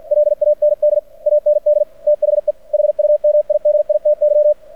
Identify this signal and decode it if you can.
Morse code, heard as a keyed tone, about 49 wpm, 600 Hz. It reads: LAND AND THE HUMIDITY
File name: archived / CRSnet